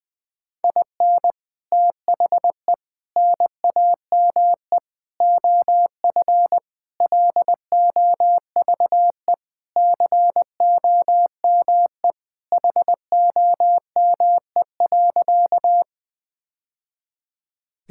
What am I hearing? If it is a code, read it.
Morse code, heard as a keyed tone, about 21 wpm, 700 Hz. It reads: IN THE NAME OF LOVE COME HOME.
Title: Sentences with 3-Word Phrases - Spoken Beginning - 20wpm